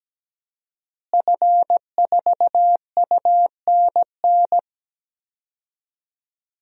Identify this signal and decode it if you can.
Morse code, heard as a keyed tone, about 17 wpm, 700 Hz. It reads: F4UNN
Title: DXCC-100 Call Signs - 5 Characters - Repeated 2x - 17wpm